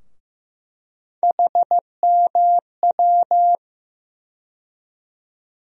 Morse code, heard as a keyed tone, about 15 wpm, 700 Hz. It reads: HMW
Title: USA Call Sign Postfixes - 3 Letters 15wpm